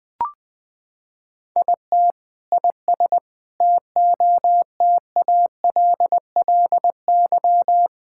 Morse code, heard as a keyed tone, about 21 wpm, 700 Hz. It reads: IT IS TOTALLY
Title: Sentences from Top 1600 Words 20wpm